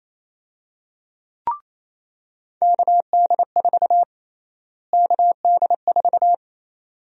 Morse code, heard as a keyed tone, about 28 wpm, 700 Hz. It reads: KD4 KD4
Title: USA Call Sign Prefixes - Repeated 2x - 28wpm